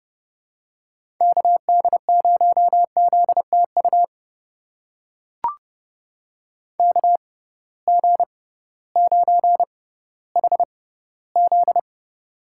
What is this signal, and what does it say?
Morse code, heard as a keyed tone, about 30 wpm, spaced as for 16 wpm, 700 Hz. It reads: KD0ZTU KG9HZ
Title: USA Call Signs - 6x Character Spacing - 30wpm